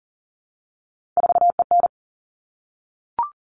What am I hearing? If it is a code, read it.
Morse code, heard as a keyed tone, about 40 wpm, 700 Hz. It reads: 4EN
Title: All Cut Numbers - 3-Digit 40wpm